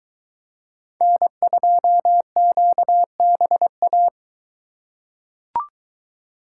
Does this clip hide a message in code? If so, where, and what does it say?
Yes: Morse code from 1.0 to 4.1 s, heard as a keyed tone, about 23 wpm, 700 Hz. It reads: N2QBA